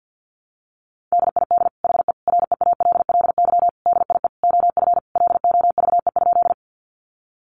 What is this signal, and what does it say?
Morse code, heard as a keyed tone, about 48 wpm, 700 Hz. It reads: DID HE REALLY DIE OF LOVE?
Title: Sentences from Top 1300 Words 50wpm